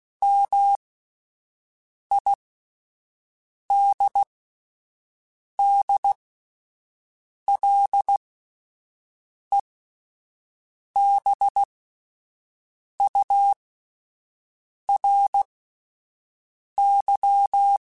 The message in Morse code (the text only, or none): MIDDLEBURY